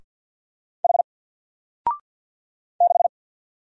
Morse code, heard as a keyed tone, about 52 wpm, 700 Hz. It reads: H 6
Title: Single Letter-Number Alt 50wpm